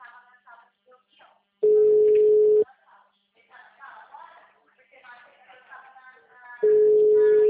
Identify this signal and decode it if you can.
Ringback tone (Europe, 425 Hz), with speech.
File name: out- -2011- - - .5414